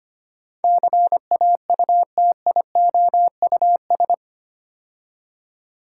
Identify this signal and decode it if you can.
Morse code, heard as a keyed tone, about 25 wpm, 700 Hz. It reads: CAUTIOUS
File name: 8 Letter Words - First 3 Letters Given - 25wpm